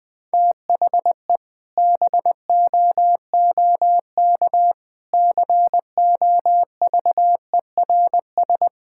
Morse code, heard as a keyed tone, about 20 wpm, 700 Hz. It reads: THE BOOK COVERS
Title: Sentences with 4-Word Phrases - Spoken Beginning - 20wpm